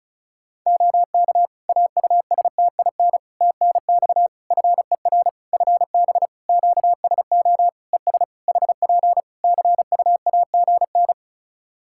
Morse code, heard as a keyed tone, about 36 wpm, 700 Hz. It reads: OK AUSTIN TNX FER FB QSO ES HP CUAGN
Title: Practice QSO - Exchange 4 - Ending - 35wpm